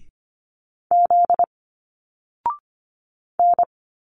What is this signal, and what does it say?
Morse code, heard as a keyed tone, about 25 wpm, 700 Hz. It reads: Z N
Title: Single Letters 25wpm